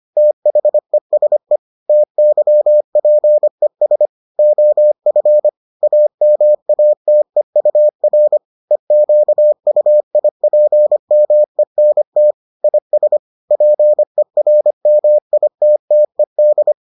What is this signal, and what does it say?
Morse code, heard as a keyed tone, about 25 wpm, 600 Hz. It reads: THESE TYPES OF AMATEUR EQUIPMENT IS PERMITTED